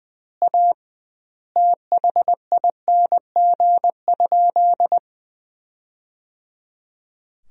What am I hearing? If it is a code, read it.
Morse code, heard as a keyed tone, about 20 wpm, 700 Hz. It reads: A THING?
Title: 2x Word-Spacing - Sentences from Top 200 Words - 20wpm